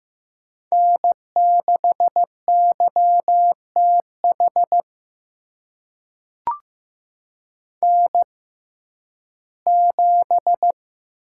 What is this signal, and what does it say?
Morse code, heard as a keyed tone, about 15 wpm, spaced as for 11 wpm, 700 Hz. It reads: N6YTH N7
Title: USA Call Signs - 5 Characters - 6x Character Spacing - 15wpm